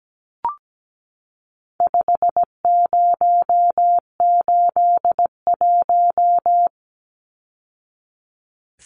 Morse code, heard as a keyed tone, about 17 wpm, 700 Hz. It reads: 5081
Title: Numbers - 4-Digit 17wpm